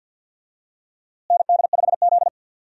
Morse code, heard as a keyed tone, about 52 wpm, 700 Hz. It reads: ND5Z
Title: USA Call Signs - 4 Characters - Repeated 2x - 50wpm